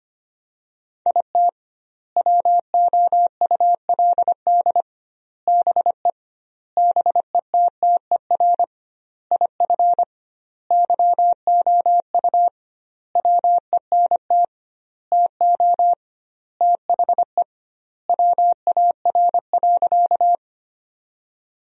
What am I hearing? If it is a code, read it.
Morse code, heard as a keyed tone, about 25 wpm, 700 Hz. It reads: IT WOULD BE BETTER IF YOU WENT TO THE WAR.
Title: 2x Word-Spacing - Sentences from Top 400 Words - 25wpm